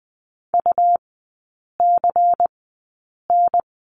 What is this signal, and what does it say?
Morse code, heard as a keyed tone, about 20 wpm, spaced as for 8 wpm, 700 Hz. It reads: UCN
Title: 107 Lesson 18e - Letter m - Three Characters 20wpm